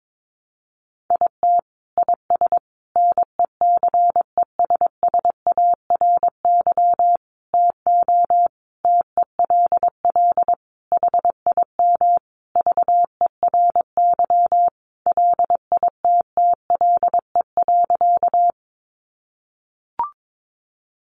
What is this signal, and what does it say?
Morse code, heard as a keyed tone, about 22 wpm, 700 Hz. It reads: IT IS NECESSARY TO TELL HIM VERY LITTLE.
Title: Sentences from Top 800 Words 22wpm